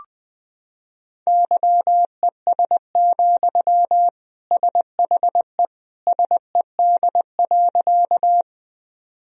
Morse code, heard as a keyed tone, about 20 wpm, 700 Hz. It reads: YES, SHE SED.
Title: Sentences from Top 500 Words and Abbreviations 20wpm